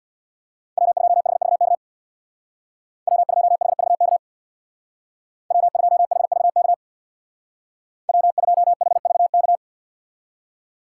Morse code, heard as a keyed tone, about 64 wpm, 700 Hz. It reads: W2HVK W2HVK W2HVK W2HVK